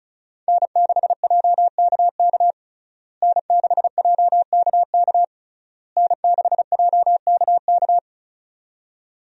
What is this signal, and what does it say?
Morse code, heard as a keyed tone, about 36 wpm, 700 Hz. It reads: N6JKK N6JKK N6JKK
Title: USA Call Signs - 5 Characters - Repeated 3x - 35wpm